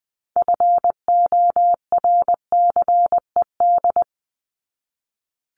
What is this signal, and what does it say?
Morse code, heard as a keyed tone, about 20 wpm, 700 Hz. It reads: FORCED
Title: ED Suffix - Top 100 Words with Base Words 20wpm